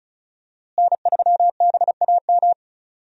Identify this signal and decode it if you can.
Morse code, heard as a keyed tone, about 36 wpm, 700 Hz. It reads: N3BAM
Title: USA Call Signs - 5 Characters - Repeated 2x - 35wpm